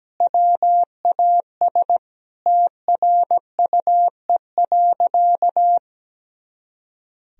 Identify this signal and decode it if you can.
Morse code, heard as a keyed tone, about 17 wpm, 700 Hz. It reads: WAS TRUE.
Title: Sentences from Top 400 Words and Abbreviations 17wpm